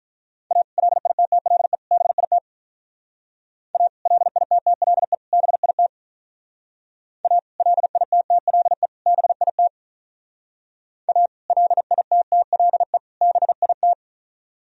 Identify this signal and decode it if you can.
Morse code, heard as a keyed tone, about 44 wpm, 700 Hz. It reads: A LITTLE BIT A LITTLE BIT A LITTLE BIT A LITTLE BIT